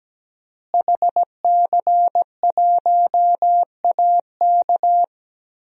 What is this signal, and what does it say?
Morse code, heard as a keyed tone, about 17 wpm, 700 Hz. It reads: HC1AK
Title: DXCC-100 Call Signs - 5 Characters - Repeated 2x - 17wpm